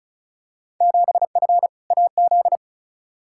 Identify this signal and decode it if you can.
Morse code, heard as a keyed tone, about 36 wpm, 690 Hz. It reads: 7F AZ
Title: Field Day Exchanges Encore - 35wpm